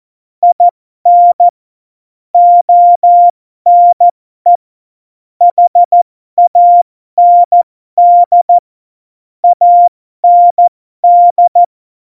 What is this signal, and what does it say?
Morse code, heard as a keyed tone, about 14 wpm, spaced as for 12 wpm, 700 Hz. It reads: IN ONE HAND AND
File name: Morsekurs / Witze